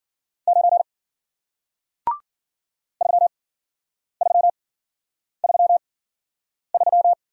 Morse code, heard as a keyed tone, about 53 wpm, spaced as for 9 wpm, 700 Hz. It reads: 9 3333